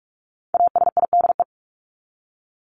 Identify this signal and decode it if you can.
Morse code, heard as a keyed tone, about 44 wpm, 700 Hz. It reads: ASIDE
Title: Top 1751-2000 Words 45wpm